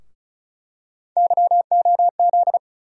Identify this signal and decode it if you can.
Morse code, heard as a keyed tone, about 34 wpm, 700 Hz. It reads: YOZ
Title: USA Call Sign Postfixes - 3 Letters 35wpm